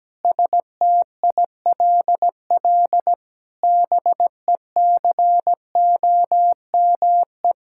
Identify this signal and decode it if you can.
Morse code, heard as a keyed tone, about 17 wpm, 700 Hz. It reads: STILL BECOME
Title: Sentences from Top 1000 Words - Review Part 6 - 17wpm